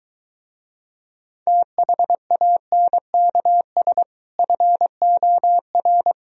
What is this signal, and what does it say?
Morse code, heard as a keyed tone, about 23 wpm, 700 Hz. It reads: THANKS FOR